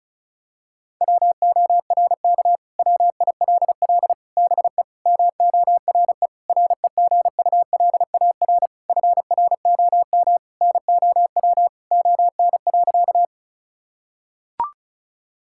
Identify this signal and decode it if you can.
Morse code, heard as a keyed tone, about 36 wpm, 700 Hz. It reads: WORK WILL BE MORE REGULAR FROM NOW ON.
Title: Sentences from Top 1000 Words - Review Part 6 - 35wpm